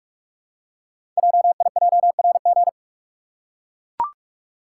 Morse code, heard as a keyed tone, about 44 wpm, 690 Hz. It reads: JIJRG
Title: Code Groups - 5 Letters - 45wpm